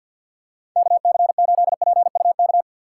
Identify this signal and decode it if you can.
Morse code, heard as a keyed tone, about 53 wpm, 700 Hz. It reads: KC8PUK
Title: USA Call Signs - 6 Characters - 8x Character Spacing - 50wpm